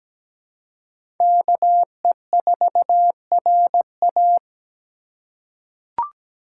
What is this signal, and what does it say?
Morse code, heard as a keyed tone, about 17 wpm, 700 Hz. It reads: KE4RA
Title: USA Call Signs - 2x Character Spacing - 17wpm